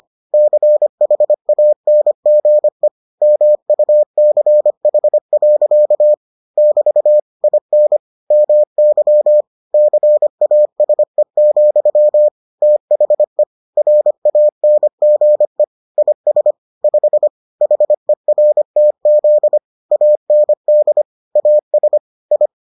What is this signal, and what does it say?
Morse code, heard as a keyed tone, about 25 wpm, 600 Hz. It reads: CHANGE MUCH. <BT> IN MY CASE, THE RANGE IS 5 HERTZ AND AS I